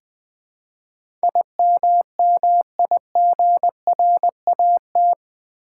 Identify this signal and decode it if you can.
Morse code, heard as a keyed tone, about 21 wpm, 700 Hz. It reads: IMMIGRAT
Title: Build A Word - 4 Letters At A Time - 11-Letter Words - 20wpm